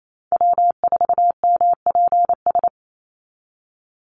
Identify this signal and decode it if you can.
Morse code, heard as a keyed tone, about 28 wpm, 700 Hz. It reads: W4MPS
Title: NA SOTA Chaser Callsigns 28wpm